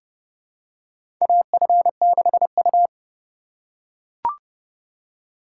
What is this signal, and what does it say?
Morse code, heard as a keyed tone, about 32 wpm, 700 Hz. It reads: AF6U
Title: USA Call Signs - 4 Characters - 4x Character Spacing - 30wpm